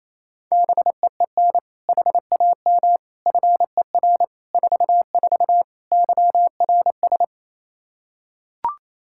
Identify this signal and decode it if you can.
Morse code, heard as a keyed tone, about 28 wpm, 700 Hz. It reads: BEEN HAM FER 44 YRS
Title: Practice QSO - Exchange 3 - More Info - 28wpm